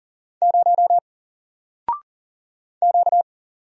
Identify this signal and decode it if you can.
Morse code, heard as a keyed tone, about 39 wpm, 700 Hz. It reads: 0 Q
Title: Single Character (Letter:Number:Punctuation:Prosign) 40wpm